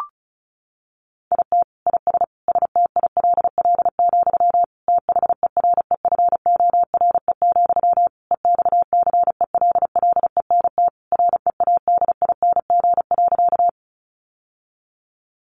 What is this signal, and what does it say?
Morse code, heard as a keyed tone, about 34 wpm, 700 Hz. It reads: IT IS STILL, THEREFORE, EXCELLENT READING.
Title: Sentences from Top 1000 Words 35wpm